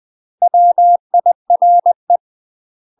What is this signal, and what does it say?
Morse code, heard as a keyed tone, about 20 wpm, 700 Hz. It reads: WIRE
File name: UK MASTER QSOs TRAINING / 20WPM / UK MASTER QSOs 0005